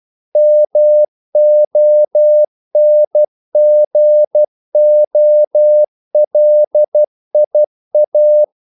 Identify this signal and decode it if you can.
Morse code, heard as a keyed tone, about 12 wpm, 600 Hz. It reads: MONGOLIA
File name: CW / Practice / NationalCapitals12